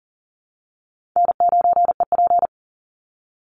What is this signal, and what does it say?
Morse code, heard as a keyed tone, about 40 wpm, 700 Hz. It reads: N9EP
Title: Easy Call Signs Repeated 2x - 40wpm